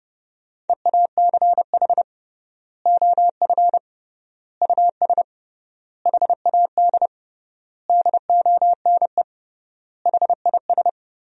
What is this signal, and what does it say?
Morse code, heard as a keyed tone, about 32 wpm, 700 Hz. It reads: EACH OF US HAD DONE HIS
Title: 3x Word-Spacing - Sentences from Top 500 Words - 30wpm